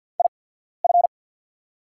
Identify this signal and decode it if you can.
Morse code, heard as a keyed tone, about 50 wpm, spaced as for 14 wpm, 700 Hz. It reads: IF
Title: 8x Character Spacing - Top 100 Words - 50wpm